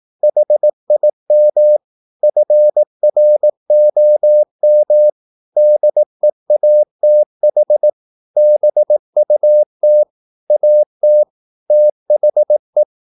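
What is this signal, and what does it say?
Morse code, heard as a keyed tone, about 18 wpm, 600 Hz. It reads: HIM FROM DEATH BUT AT THE
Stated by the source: Morse practice at 18 wpm, English text, from CW med Gnister.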